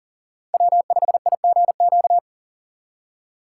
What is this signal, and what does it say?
Morse code, heard as a keyed tone, about 40 wpm, 700 Hz. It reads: W5IGQ
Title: USA Call Signs - 5 Characters - 4x Character Spacing - 40wpm